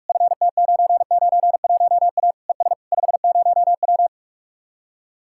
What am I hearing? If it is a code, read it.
Morse code, heard as a keyed tone, about 59 wpm, spaced as for 52 wpm, 700 Hz. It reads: FT991A ES 50W
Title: Speed-Racing Common QSO Phrases - 45wpm